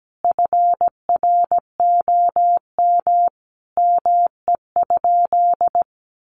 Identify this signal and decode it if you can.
Morse code, heard as a keyed tone, about 17 wpm, 700 Hz. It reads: FROM ME?
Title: Sentences from Top 1500 Words 17wpm